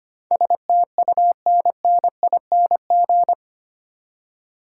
Morse code, heard as a keyed tone, about 25 wpm, 700 Hz. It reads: STUNNING
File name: Eight Letter Words - 25wpm